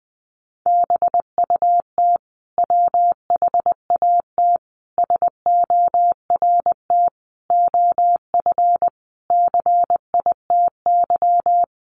Morse code, heard as a keyed tone, about 20 wpm, 700 Hz. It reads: BUT WHAT SORT OF CITY
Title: Sentences from Top 1300 Words 20wpm